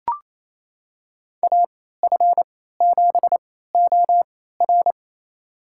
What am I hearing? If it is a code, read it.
Morse code, heard as a keyed tone, about 28 wpm, spaced as for 16 wpm, 700 Hz. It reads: AF7OR
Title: USA Call Signs - 3x Character Spacing - 28wpm